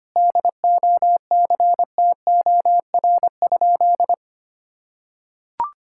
Morse code, heard as a keyed tone, about 25 wpm, 700 Hz. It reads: DOCTOR?
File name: practice / 25wpm / Sentences with 4-Word Phrases - 25wpm